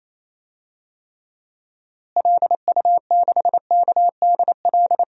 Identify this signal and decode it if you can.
Morse code, heard as a keyed tone, about 28 wpm, 700 Hz. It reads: LU6KDL
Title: DXCC-100 Call Signs - 6 Characters - Repeated 3x - 28wpm